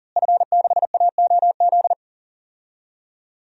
Morse code, heard as a keyed tone, about 40 wpm, 700 Hz. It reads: F6AOZ
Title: DXCC-100 Call Signs - 5 Characters - Repeated 3x - 40wpm